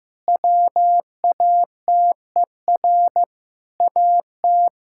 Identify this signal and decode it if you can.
Morse code, heard as a keyed tone, about 15 wpm, 700 Hz. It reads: WATER AT
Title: Sentences from Top 1000 Words - Review Part 6 - 15wpm